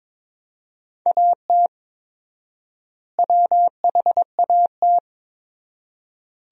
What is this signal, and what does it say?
Morse code, heard as a keyed tone, about 22 wpm, 700 Hz. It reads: AT WHAT
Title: 4x Word-Spacing - Sentences from Top 100 Words - 22wpm